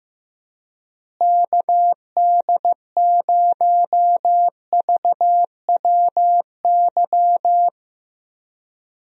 Morse code, heard as a keyed tone, about 15 wpm, 700 Hz. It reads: KD0VWY
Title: USA Call Signs - 6 Characters - Repeated 2x - 15wpm